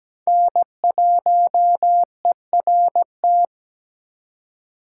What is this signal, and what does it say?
Morse code, heard as a keyed tone, about 20 wpm, 700 Hz. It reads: N1ERT
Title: Speed Racing - USA Call Signs 17wpm